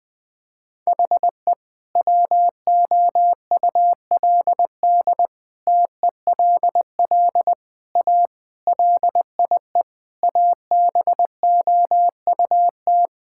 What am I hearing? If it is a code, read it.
Morse code, heard as a keyed tone, about 20 wpm, 700 Hz. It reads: HE WOULD TELL A LIE ABOUT
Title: Sentences from Top 1600 Words 20wpm